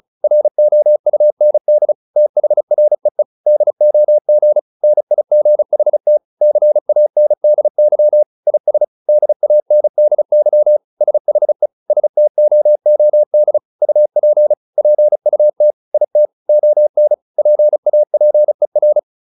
Morse code, heard as a keyed tone, about 34 wpm, 600 Hz. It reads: ROUND THREE DOG NIGHT CANDY IS DANDY SHE STOOD UP PUT IT ON PAPER
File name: PR401 35